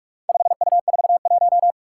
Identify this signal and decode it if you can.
Morse code, heard as a keyed tone, about 45 wpm, 700 Hz. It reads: 5U41
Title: Four Challenging Letter-Number Combinations 45wpm